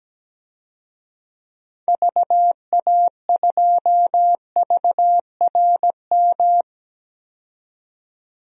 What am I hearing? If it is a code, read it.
Morse code, heard as a keyed tone, about 17 wpm, 700 Hz. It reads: VA2VRM
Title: DXCC-100 Call Signs - 6 Characters - 17wpm